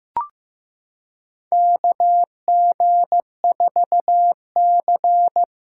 Morse code, heard as a keyed tone, about 15 wpm, 700 Hz. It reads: KG4C
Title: USA Callsign DE Callsign - 4 Characters - 15wpm